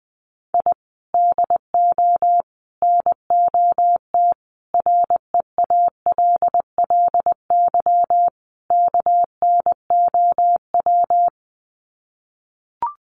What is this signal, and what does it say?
Morse code, heard as a keyed tone, about 20 wpm, 700 Hz. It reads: I DO NOT REALLY KNOW
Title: Sets of 5 Words Encore 20wpm